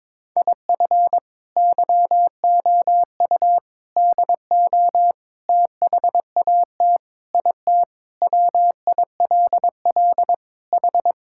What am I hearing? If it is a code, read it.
Morse code, heard as a keyed tone, about 22 wpm, 700 Hz. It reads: IF YOU DO THAT IT WILL H
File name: practice / 22wpm / Sentences with 3-Word Phrases - 22wpm